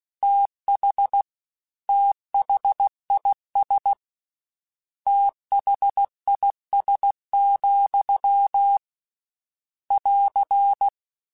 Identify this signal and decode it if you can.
Morse code, heard as a keyed tone, about 16 wpm, 800 Hz. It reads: TH THIS THIS, <AR>
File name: Combinations / Combinations-mp3s / Combinations@15